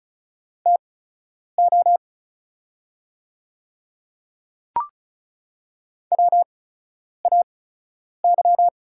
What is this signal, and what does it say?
Morse code, heard as a keyed tone, about 36 wpm, spaced as for 10 wpm, 700 Hz. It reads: TO WAY
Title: 8x Character Spacing - Top 100 Words - 35wpm